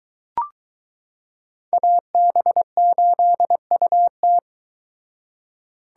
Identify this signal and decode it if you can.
Morse code, heard as a keyed tone, about 20 wpm, 700 Hz. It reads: AB8UT